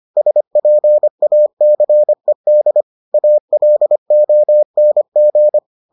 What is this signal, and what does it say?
Morse code, heard as a keyed tone, about 25 wpm, 600 Hz. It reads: SPACED ALONG